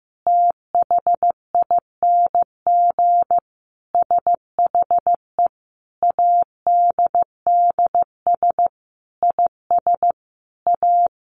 Morse code, heard as a keyed tone, about 15 wpm, 700 Hz. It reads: THING SHE ADDS IS A
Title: Easy Phrases 15wpm